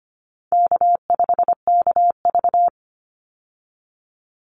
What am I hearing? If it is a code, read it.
Morse code, heard as a keyed tone, about 25 wpm, 700 Hz. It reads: K5KV
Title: Easy Call Signs - 25wpm